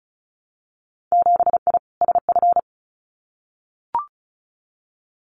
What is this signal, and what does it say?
Morse code, heard as a keyed tone, about 34 wpm, 700 Hz. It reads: 7I SF
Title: Winter Field Day Exchanges Encore 35wpm